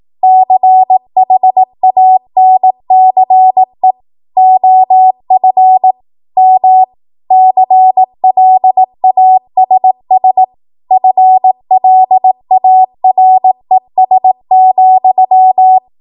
Morse code, heard as a keyed tone, about 18 wpm, 750 Hz. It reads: CHANCE OF M CLASS FLARES,
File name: Morse / 18WPM